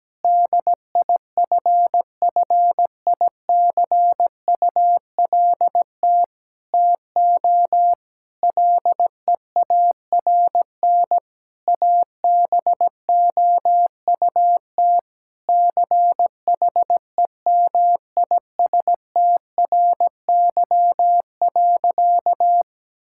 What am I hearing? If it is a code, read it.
Morse code, heard as a keyed tone, about 17 wpm, 700 Hz. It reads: DIFFICULT TO LEARN ABOUT CHEMISTRY.
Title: Sentences from Top 1000 Words - Review Part 6 - 17wpm